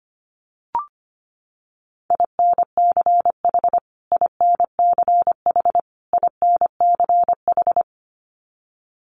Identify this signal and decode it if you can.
Morse code, heard as a keyed tone, about 25 wpm, 700 Hz. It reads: INCH INCH INCH
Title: Top 500 Words Repeated 25wpm